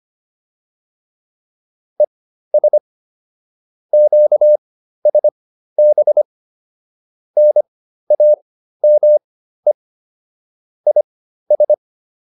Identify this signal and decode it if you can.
Morse code, heard as a keyed tone, about 25 wpm, spaced as for 13 wpm, 600 Hz. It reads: ES QSB NAME IS
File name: QSO405 13